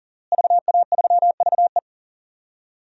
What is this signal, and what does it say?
Morse code, heard as a keyed tone, about 40 wpm, 700 Hz. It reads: VA3VE
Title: DXCC-100 Call Signs - 5 Characters - Repeated 3x - 40wpm